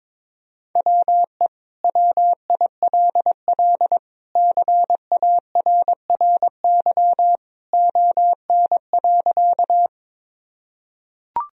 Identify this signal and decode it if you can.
Morse code, heard as a keyed tone, about 22 wpm, 700 Hz. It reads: WE WILL CARRY ON.
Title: Sentences from Top 1000 Words - Review Part 1 - 22wpm